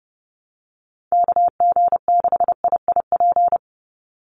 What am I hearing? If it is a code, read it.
Morse code, heard as a keyed tone, about 29 wpm, 700 Hz. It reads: KG6IIP
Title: Call Signs - 30wpm